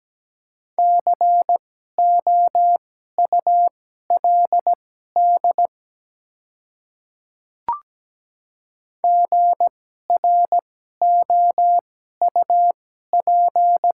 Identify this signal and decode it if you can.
Morse code, heard as a keyed tone, about 17 wpm, spaced as for 12 wpm, 700 Hz. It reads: COULD GROUP
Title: 2x Character Spacing - Top 300 Words - 17wpm